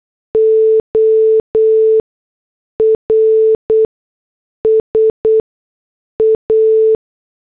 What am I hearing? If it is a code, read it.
Morse code, heard as a keyed tone, about 8 wpm, 440 Hz. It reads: O R S A